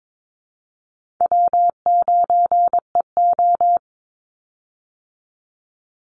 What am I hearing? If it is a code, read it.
Morse code, heard as a keyed tone, about 22 wpm, 700 Hz. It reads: W9EO
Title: Call Signs Repeated 2x - 22wpm